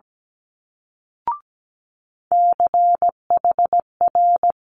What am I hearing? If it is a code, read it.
Morse code, heard as a keyed tone, about 17 wpm, 700 Hz. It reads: CHR
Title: Multi-letter Phonemes Quick Fire 17wpm